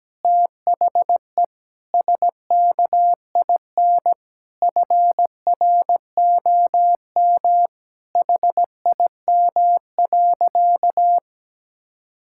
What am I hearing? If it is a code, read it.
Morse code, heard as a keyed tone, about 17 wpm, 700 Hz. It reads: THE SKIN FROM HIM.
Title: Sentences from Top 1600 Words 17wpm